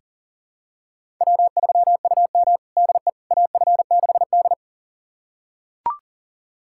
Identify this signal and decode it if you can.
Morse code, heard as a keyed tone, about 40 wpm, 690 Hz. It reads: W3UM DE AF6D
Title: USA Callsign DE Callsign - 4 Characters - 40wpm